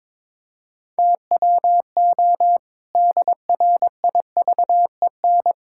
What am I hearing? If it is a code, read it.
Morse code, heard as a keyed tone, about 22 wpm, 700 Hz. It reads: TWO DRIVEN
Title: Pangram Challenge v2 - 22wpm